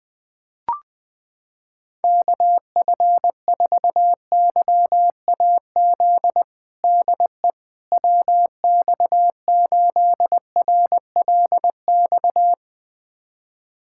Morse code, heard as a keyed tone, about 21 wpm, 700 Hz. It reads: KF4YAZ DE WX8RLX
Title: USA Callsign DE Callsign 20wpm